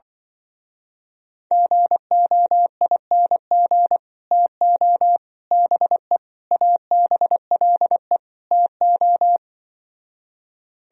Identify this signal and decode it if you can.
Morse code, heard as a keyed tone, about 24 wpm, 700 Hz. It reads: GOING TO BE ABLE TO
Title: Speed-Racing - Sets of 5 Words 20wpm